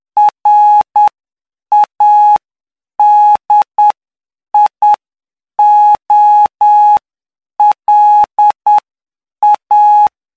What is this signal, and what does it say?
Morse code, heard as a keyed tone, about 9 wpm, spaced as for 7 wpm, 840 Hz. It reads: RADIOLA
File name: cw / py2lcd / 11-PY2LCD-Li��o 20